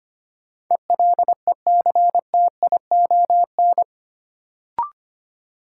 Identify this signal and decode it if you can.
Morse code, heard as a keyed tone, about 25 wpm, 700 Hz. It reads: ELECTION
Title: Sentences from Top 1500 Words - New Word Drill Alt - 25wpm